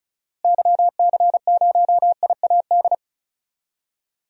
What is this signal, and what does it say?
Morse code, heard as a keyed tone, about 34 wpm, 700 Hz. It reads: YC0IAD
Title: DXCC-100 Call Signs - 6 Characters - Repeated 2x - 35wpm